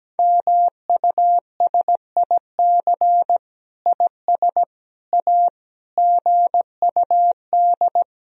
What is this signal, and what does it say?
Morse code, heard as a keyed tone, about 17 wpm, 700 Hz. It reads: MUSIC IS A GUD
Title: Sentences from Top 300 Words and Abbreviations 17wpm